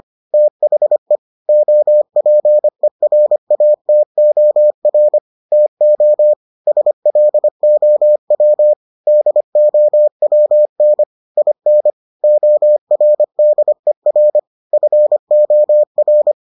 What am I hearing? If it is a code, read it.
Morse code, heard as a keyed tone, about 25 wpm, 600 Hz. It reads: THE OPERATOR TO SLOW DOWN IN ORDER FOR